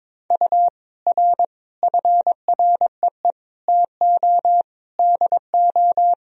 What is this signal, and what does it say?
Morse code, heard as a keyed tone, about 21 wpm, 700 Hz. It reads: U R FREE TO DO
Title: Sentences from Top 500 Words and Abbreviations 22wpm